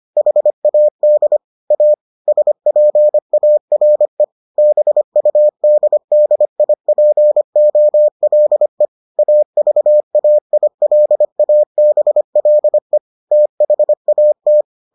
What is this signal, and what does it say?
Morse code, heard as a keyed tone, about 25 wpm, 600 Hz. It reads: HAD A SPARE BUDDIPOLE AVAILABLE THAT